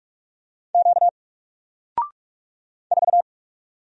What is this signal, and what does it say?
Morse code, heard as a keyed tone, about 43 wpm, 700 Hz. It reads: Q 4
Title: Single Character (Letter:Number:Punctuation:Prosign) Alt 45wpm